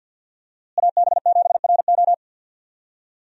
Morse code, heard as a keyed tone, about 52 wpm, 700 Hz. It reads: AB7RO
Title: USA Call Signs - 5 Characters - Repeated 3x - 50wpm